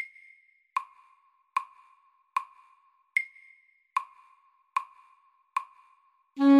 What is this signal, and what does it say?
There is no code.